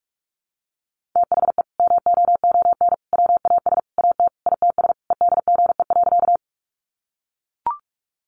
Morse code, heard as a keyed tone, about 44 wpm, 700 Hz. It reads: THE MOON WAS AT ITS EDGE.
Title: Sentences from Top 1300 Words 45wpm